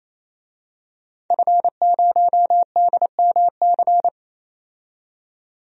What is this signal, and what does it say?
Morse code, heard as a keyed tone, about 28 wpm, 700 Hz. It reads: F0DMC
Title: DXCC-100 Call Signs - 5 Characters - 28wpm